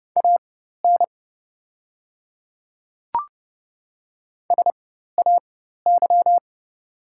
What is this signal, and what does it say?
Morse code, heard as a keyed tone, about 30 wpm, spaced as for 14 wpm, 700 Hz. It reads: AN SAY
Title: 4x Character Spacing - Top 200 Words - 30wpm